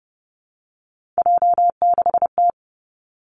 Morse code, heard as a keyed tone, about 30 wpm, 700 Hz. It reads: J6T
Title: Three Challenging Letter-Number Combinations 30wpm